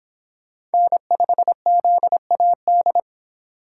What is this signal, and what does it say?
Morse code, heard as a keyed tone, about 24 wpm, 700 Hz. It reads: N5ZAD